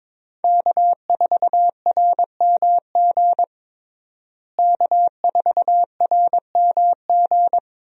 Morse code, heard as a keyed tone, about 22 wpm, 700 Hz. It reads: K4RMG K4RMG
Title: USA Call Signs - 5 Characters - Repeated 3x - 22wpm